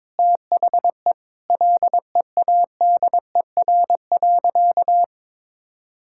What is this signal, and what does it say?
Morse code, heard as a keyed tone, about 22 wpm, 700 Hz. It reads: THE LEADER.
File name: practice / 22wpm / Sentences with 3-Word Phrases - 22wpm